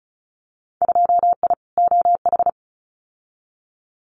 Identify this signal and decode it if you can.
Morse code, heard as a keyed tone, about 36 wpm, 700 Hz. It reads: 2I OH
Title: Winter Field Day Exchanges 35wpm